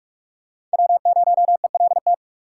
Morse code, heard as a keyed tone, about 47 wpm, 690 Hz. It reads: W0ELT
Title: USA Call Signs - 5 Characters - 6x Character Spacing - 45wpm